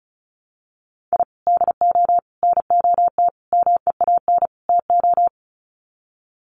Morse code, heard as a keyed tone, about 34 wpm, 700 Hz. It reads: I DO NOT MEAN TO
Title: Sets of 5 Words 35wpm